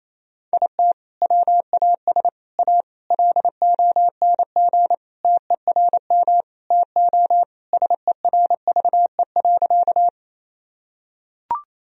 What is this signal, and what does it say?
Morse code, heard as a keyed tone, about 28 wpm, 700 Hz. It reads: IT WAS A LONG TERM TO SERVE.
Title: Sentences from Top 1000 Words - Review Part 4 - 28wpm